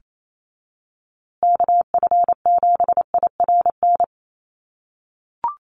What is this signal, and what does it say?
Morse code, heard as a keyed tone, about 28 wpm, 700 Hz. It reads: KF7IRN
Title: USA Call Signs Repeated 2x 28wpm